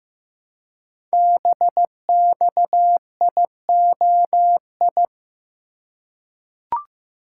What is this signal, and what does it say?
Morse code, heard as a keyed tone, about 15 wpm, 700 Hz. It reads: BXIOI
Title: Code Groups - 5 Letters - 15wpm